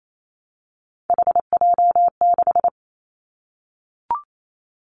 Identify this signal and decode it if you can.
Morse code, heard as a keyed tone, about 28 wpm, 700 Hz. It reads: HJ6
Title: Callsign Prefixes for Top 100 DXCC - 28wpm